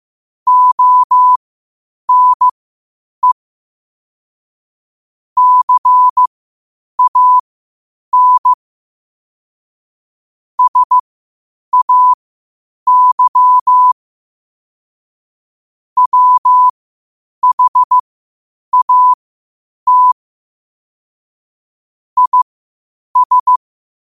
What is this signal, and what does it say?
Morse code, heard as a keyed tone, about 15 wpm, spaced as for 8 wpm, 1000 Hz. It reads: ONE CAN SAY WHAT IS
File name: qotdfiles / 8wpm